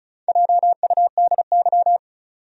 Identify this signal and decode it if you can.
Morse code, heard as a keyed tone, about 36 wpm, 700 Hz. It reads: JUDY
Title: USA Common Names - Max 5 Letters - 35wpm